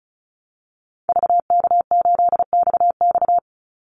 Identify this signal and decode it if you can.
Morse code, heard as a keyed tone, about 34 wpm, 700 Hz. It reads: VK8XX
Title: Call Signs - 35wpm